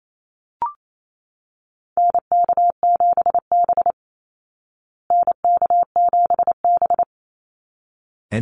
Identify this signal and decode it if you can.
Morse code, heard as a keyed tone, about 28 wpm, 700 Hz. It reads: NK7B NK7B
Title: Call Signs Repeated 3x - 28wpm